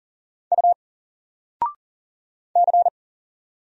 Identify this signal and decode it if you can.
Morse code, heard as a keyed tone, about 39 wpm, 700 Hz. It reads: U C
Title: Single Character (Letter:Number:Punctuation:Prosign) 40wpm